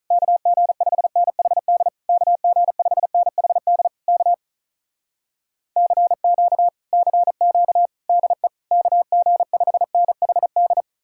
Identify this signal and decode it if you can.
Morse code, heard as a keyed tone, about 47 wpm, 700 Hz. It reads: KG5NHD KG5NHD K CQ CQ DE KG5NHD